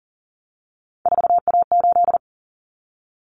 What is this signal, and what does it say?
Morse code, heard as a keyed tone, about 40 wpm, 700 Hz. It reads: 4A8
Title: Common Cut Numbers - 3-Digit 40wpm